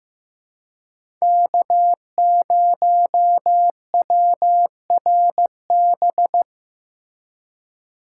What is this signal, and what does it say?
Morse code, heard as a keyed tone, about 15 wpm, 700 Hz. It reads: K0WRB
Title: USA Call Signs - 5 Characters - 4x Character Spacing - 15wpm